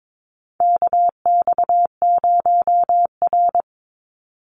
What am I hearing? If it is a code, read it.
Morse code, heard as a keyed tone, about 22 wpm, 700 Hz. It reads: KX0R
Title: NA SOTA Activator Callsigns 22wpm